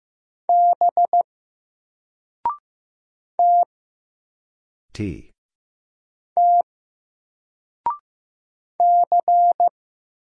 Morse code, heard as a keyed tone, about 15 wpm, 700 Hz. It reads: B T T C
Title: Single Letter-Number 15wpm